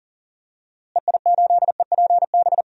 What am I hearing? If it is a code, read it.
Morse code, heard as a keyed tone, about 37 wpm, 700 Hz. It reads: EI8EPB
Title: DXCC-100 Call Signs - 6 Characters - Repeated 2x - 40wpm